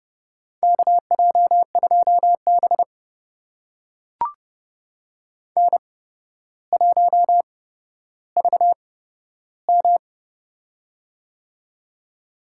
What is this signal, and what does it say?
Morse code, heard as a keyed tone, about 30 wpm, spaced as for 13 wpm, 700 Hz. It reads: KJ2B N1VM
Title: USA Call Signs - 4 Characters - 8x Character Spacing - 30wpm